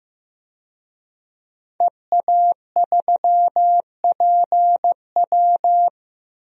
Morse code, heard as a keyed tone, about 15 wpm, 700 Hz. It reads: EA3PW
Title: DXCC-100 Call Signs - 5 Characters - 15wpm